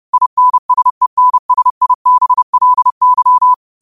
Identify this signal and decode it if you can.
Morse code, heard as a keyed tone, about 30 wpm, 1000 Hz. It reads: INSENSIBLY